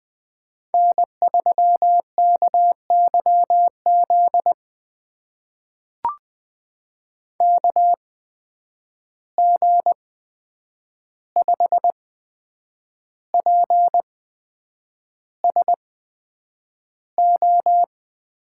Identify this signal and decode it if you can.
Morse code, heard as a keyed tone, about 20 wpm, spaced as for 8 wpm, 700 Hz. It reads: N3KYZ KG5PSO
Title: USA Call Signs - 8x Character Spacing - 20wpm